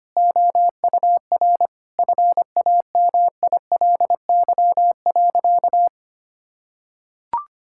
Morse code, heard as a keyed tone, about 25 wpm, 700 Hz. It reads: OUR FAMILY.
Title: Sentences from Top 1000 Words - Review Part 6 - 25wpm